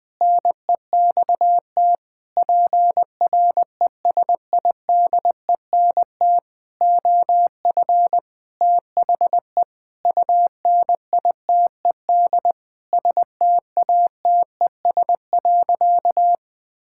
Morse code, heard as a keyed tone, about 19 wpm, 700 Hz. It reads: NEXT PRESIDENT OF THE UNITED STATES.
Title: Sentences with 8-Word Phrases - 20wpm